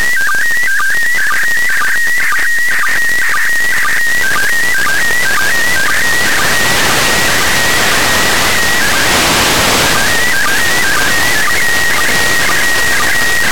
SSTV (slow-scan television), PD 120.
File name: ariss-sstv / partial-ariss- -1624